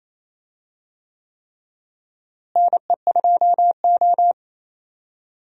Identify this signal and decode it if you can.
Morse code, heard as a keyed tone, about 28 wpm, 700 Hz. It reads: NE2O
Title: USA Call Signs - 4 Characters - Repeated 3x - 28wpm